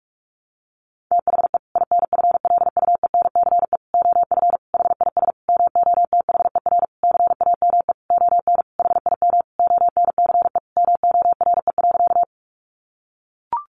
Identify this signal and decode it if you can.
Morse code, heard as a keyed tone, about 44 wpm, 700 Hz. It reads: THE INFLUENCE OF HIS MOTHER CAME ON HIM ONCE MORE.
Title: Sentences from Top 1100 Words 45wpm